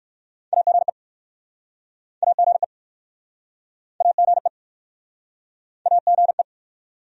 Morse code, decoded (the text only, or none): AGE AGE AGE AGE